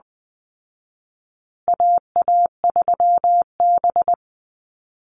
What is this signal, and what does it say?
Morse code, heard as a keyed tone, about 20 wpm, 700 Hz. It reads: AA3B
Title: Call Signs Repeated 2x - 20wpm